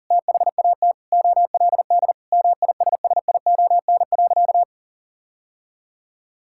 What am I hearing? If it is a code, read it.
Morse code, heard as a keyed tone, about 40 wpm, 700 Hz. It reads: THAT OLD MISSION.
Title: Sentences from Top 1600 Words 40wpm